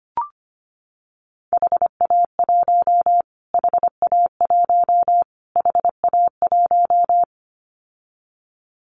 Morse code, heard as a keyed tone, about 25 wpm, 700 Hz. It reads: HA1 HA1 HA1
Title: Callsign Prefixes for Top 25 DXCC - Repeated 3x - 25wpm